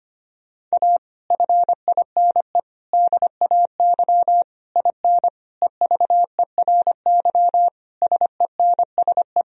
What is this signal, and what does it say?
Morse code, heard as a keyed tone, about 25 wpm, 700 Hz. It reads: A FINE DAY IN EVERY SENSE
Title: Sentences from Top 1000 Words - Review Part 4 - 25wpm